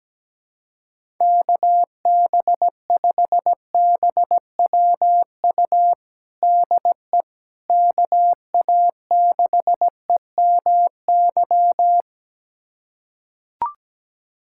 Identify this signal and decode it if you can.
Morse code, heard as a keyed tone, about 17 wpm, 700 Hz. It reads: KB5BWU DE KA6EMY
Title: USA Callsign DE Callsign - 6 Characters - 17wpm